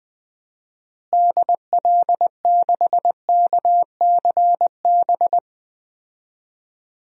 Morse code, heard as a keyed tone, about 20 wpm, 700 Hz. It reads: DL6KCB
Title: DXCC-100 Call Signs - 6 Characters - Repeated 3x - 20wpm